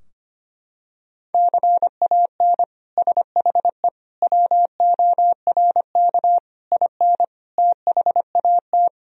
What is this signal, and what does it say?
Morse code, heard as a keyed tone, about 25 wpm, 700 Hz. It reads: CAN SHE WORK IN THAT
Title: Sentences from Top 1000 Words - Review Part 4 - 25wpm